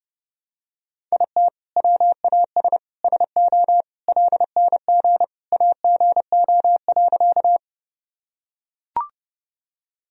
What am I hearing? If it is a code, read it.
Morse code, heard as a keyed tone, about 32 wpm, 700 Hz. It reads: IT WAS SO LNG AGO.
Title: Sentences from Top 500 Words and Abbreviations 30wpm